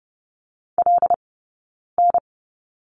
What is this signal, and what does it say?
Morse code, heard as a keyed tone, about 30 wpm, 700 Hz. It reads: L N
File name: Lesson 18d - Letter m - Two Characters 30wpm